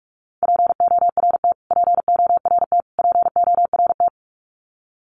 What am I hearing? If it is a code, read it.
Morse code, heard as a keyed tone, about 48 wpm, 700 Hz. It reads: PORT PORT PORT
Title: Top 200 Words Repeated 45wpm